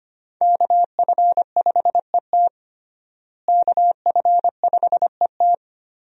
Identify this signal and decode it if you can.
Morse code, heard as a keyed tone, about 25 wpm, 700 Hz. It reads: KF5ET KF5ET
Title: USA Call Signs - 5 Characters - Repeated 2x - 25wpm